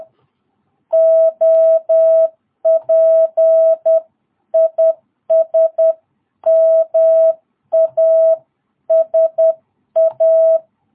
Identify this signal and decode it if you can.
Morse code, heard as a keyed tone, about 10 wpm, 660 Hz. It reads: OP IS MASA
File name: jh8cbh / 2022cw